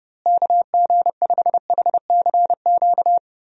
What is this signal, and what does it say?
Morse code, heard as a keyed tone, about 31 wpm, 700 Hz. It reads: KG5HCQ
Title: USA Call Signs - 6 Characters - Repeated 2x - 30wpm